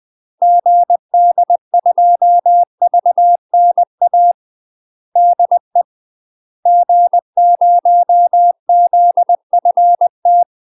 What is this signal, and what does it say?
Morse code, heard as a keyed tone, about 20 wpm, 700 Hz. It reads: GD2VNA DE G0ZFT
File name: UK MASTER QSOs TRAINING / 20WPM / UK MASTER QSOs 0038